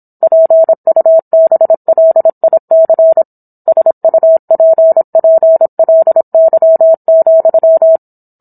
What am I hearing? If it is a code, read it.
Morse code, heard as a keyed tone, about 27 wpm, 650 Hz. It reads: PUBLIC SUPPLY,